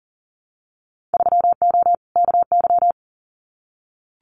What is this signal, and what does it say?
Morse code, heard as a keyed tone, about 41 wpm, 700 Hz. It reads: 3O KY
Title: Winter Field Day Exchanges 40wpm